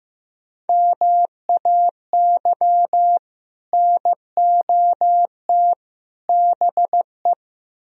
Morse code, heard as a keyed tone, about 17 wpm, 700 Hz. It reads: MAY NOT BE
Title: Speed-Racing - Sets of 3 Words 15wpm